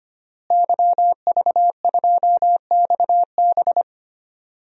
Morse code, heard as a keyed tone, about 25 wpm, 700 Hz. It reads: YV2XB
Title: DXCC-100 Call Signs - 5 Characters - Repeated 2x - 25wpm